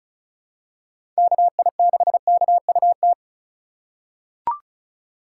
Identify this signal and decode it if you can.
Morse code, heard as a keyed tone, about 36 wpm, 700 Hz. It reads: KI6KUT
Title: USA Call Signs - 6 Characters - 2x Character Spacing - 35wpm